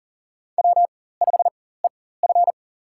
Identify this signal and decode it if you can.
Morse code, heard as a keyed tone, about 40 wpm, spaced as for 19 wpm, 700 Hz. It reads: W5EF
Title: USA Call Signs - 4x Character Spacing - 40wpm